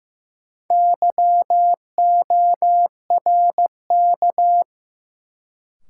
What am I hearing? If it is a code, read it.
Morse code, heard as a keyed tone, about 15 wpm, 700 Hz. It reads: YORK
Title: Sets of 2 Words - New Word Drill - 15wpm